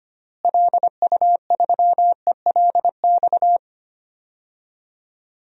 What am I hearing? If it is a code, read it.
Morse code, heard as a keyed tone, about 25 wpm, 700 Hz. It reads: LU3ELX
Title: DXCC-100 Call Signs - 6 Characters - Repeated 3x - 25wpm